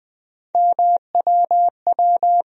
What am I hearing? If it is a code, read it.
Morse code, heard as a keyed tone, about 20 wpm, 700 Hz. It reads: MWW
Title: USA Call Sign Postfixes - 3 Letters 20wpm